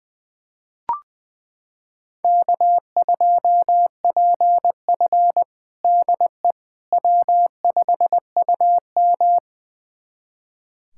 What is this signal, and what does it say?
Morse code, heard as a keyed tone, about 20 wpm, 700 Hz. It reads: K2PF DE W5UM
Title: USA Callsign DE Callsign - 4 Characters - 20wpm